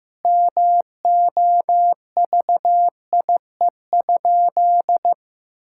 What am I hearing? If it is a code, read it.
Morse code, heard as a keyed tone, about 15 wpm, 700 Hz. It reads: MOVIE?
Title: Sentences from Top 1000 Words - Review Part 5 - 15wpm